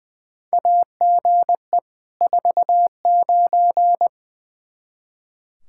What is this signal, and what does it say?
Morse code, heard as a keyed tone, about 21 wpm, 700 Hz. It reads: AGE 49
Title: Practice QSO - Exchange 3 - More Info - 20wpm